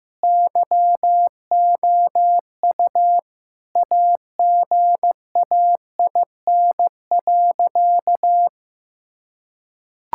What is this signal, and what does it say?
Morse code, heard as a keyed tone, about 15 wpm, 700 Hz. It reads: YOU AGAIN.
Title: Sentences from Top 1600 Words 15wpm